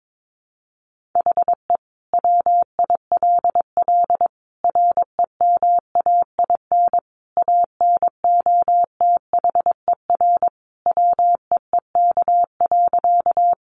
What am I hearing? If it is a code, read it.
Morse code, heard as a keyed tone, about 22 wpm, 700 Hz. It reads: HE WILL REMAIN ANOTHER WEEK.
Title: Sentences from Top 1200 Words 22wpm